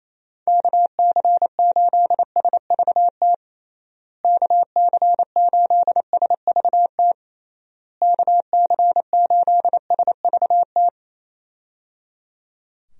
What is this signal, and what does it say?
Morse code, heard as a keyed tone, about 27 wpm, 700 Hz. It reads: KC8SVT KC8SVT KC8SVT
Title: USA Call Signs - 6 Characters - Repeated 3x - 28wpm